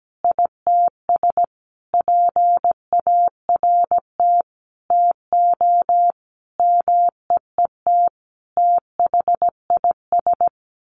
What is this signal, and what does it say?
Morse code, heard as a keyed tone, about 17 wpm, 700 Hz. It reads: ITS PART TO MEET THIS
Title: Sentences from Top 1100 Words 17wpm